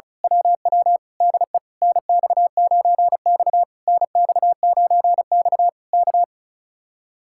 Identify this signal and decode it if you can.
Morse code, heard as a keyed tone, about 32 wpm, 700 Hz. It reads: WW DE NX9X NX9X K
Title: Speed-Racing Common QSO Phrases - 35wpm